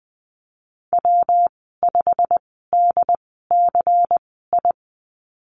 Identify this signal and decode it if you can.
Morse code, heard as a keyed tone, about 20 wpm, spaced as for 15 wpm, 700 Hz. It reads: W5DCI
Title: 140 Lesson 23g - Letter f - Call Signs 20wpm